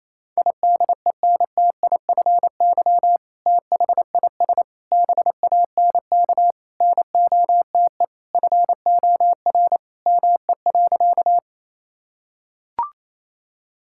Morse code, heard as a keyed tone, about 28 wpm, 700 Hz. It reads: IDENTIFY THIS BANK NOTE FOR ME.
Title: Sentences from Top 1000 Words - Review Part 6 - 28wpm